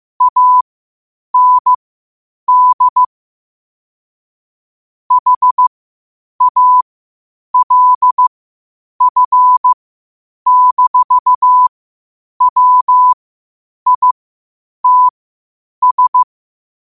Morse code, heard as a keyed tone, about 15 wpm, spaced as for 8 wpm, 1000 Hz. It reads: AND HALF-WITS
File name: qotdfiles / 8wpm